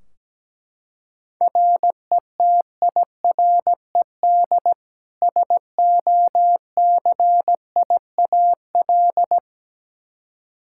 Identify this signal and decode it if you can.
Morse code, heard as a keyed tone, about 17 wpm, 700 Hz. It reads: RETIRED SOCIAL
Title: K1USN SST QSO Practice 17wpm